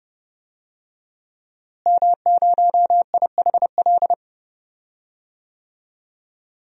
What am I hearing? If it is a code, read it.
Morse code, heard as a keyed tone, about 32 wpm, 700 Hz. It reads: M0IHL
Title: DXCC-100 Call Signs - 5 Characters - Repeated 3x - 30wpm